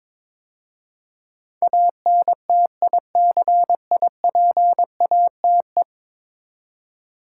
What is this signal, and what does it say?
Morse code, heard as a keyed tone, about 22 wpm, 700 Hz. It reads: ANTICIPATE
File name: practice / 22wpm / 10 Letter Words - First 3 Letters Given - 22wpm